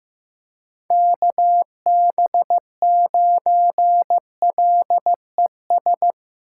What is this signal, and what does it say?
Morse code, heard as a keyed tone, about 15 wpm, 700 Hz. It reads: KB9LES
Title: USA Call Signs - 6x Character Spacing - 15wpm